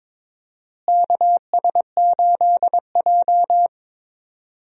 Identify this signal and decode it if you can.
Morse code, heard as a keyed tone, about 22 wpm, 700 Hz. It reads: KS8J
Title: USA Call Signs - 4 Characters - Repeated 3x - 22wpm